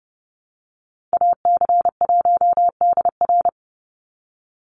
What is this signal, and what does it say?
Morse code, heard as a keyed tone, about 30 wpm, 700 Hz. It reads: AC1DR
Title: USA Call Signs 30wpm